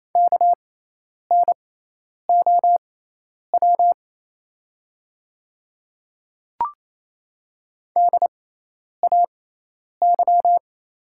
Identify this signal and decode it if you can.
Morse code, heard as a keyed tone, about 28 wpm, spaced as for 10 wpm, 700 Hz. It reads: KNOW DAY
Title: 6x Character Spacing - Top 100 Words - 28wpm